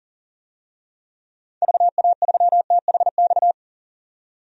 Morse code, heard as a keyed tone, about 39 wpm, 700 Hz. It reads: VA3THX
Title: DXCC-100 Call Signs - 6 Characters - 40wpm